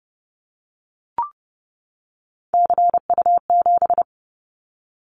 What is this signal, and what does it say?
Morse code, heard as a keyed tone, about 30 wpm, 700 Hz. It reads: CU7
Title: Callsign Prefixes for Top 50 DXCC - 30wpm